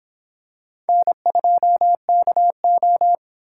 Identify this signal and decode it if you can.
Morse code, heard as a keyed tone, about 25 wpm, 700 Hz. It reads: N2KO